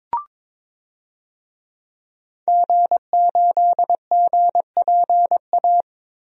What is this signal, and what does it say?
Morse code, heard as a keyed tone, about 22 wpm, 700 Hz. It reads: G8GPA
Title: DXCC-100 Call Signs - 5 Characters - Repeated 2x - 22wpm